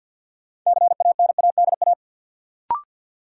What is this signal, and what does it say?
Morse code, heard as a keyed tone, about 53 wpm, 700 Hz. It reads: CANADA
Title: Sets of 5 Words - New Word Drill - 50wpm